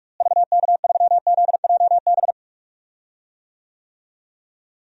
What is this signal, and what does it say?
Morse code, heard as a keyed tone, about 46 wpm, 700 Hz. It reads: VK3ZJB
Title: DXCC-100 Call Signs - 6 Characters - Repeated 2x - 45wpm